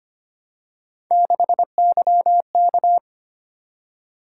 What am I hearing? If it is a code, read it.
Morse code, heard as a keyed tone, about 25 wpm, 700 Hz. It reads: 6YK
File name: Three Letter-Number Combinations - 25wpm